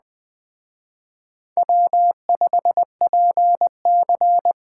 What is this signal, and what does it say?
Morse code, heard as a keyed tone, about 20 wpm, 700 Hz. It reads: W5PC
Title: Speed Racing - USA 1x2 Call Signs 17wpm